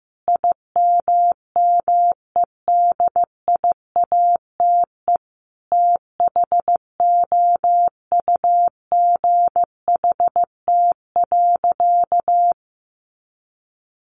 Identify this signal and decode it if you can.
Morse code, heard as a keyed tone, about 15 wpm, 700 Hz. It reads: IMMEDIATE THOUGHT.
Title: Sentences from Top 1500 Words 15wpm